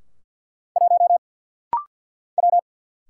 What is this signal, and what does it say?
Morse code, heard as a keyed tone, about 50 wpm, 700 Hz. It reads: 1 W
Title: Single Letter-Number Alt - Rapid-Fire 50wpm